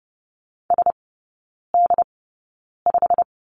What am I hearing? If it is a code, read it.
Morse code, heard as a keyed tone, about 30 wpm, spaced as for 9 wpm, 700 Hz. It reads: SD5
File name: course / 30:10wpm / 087 Lesson 15e - Number 5 - Three Characters 30wpm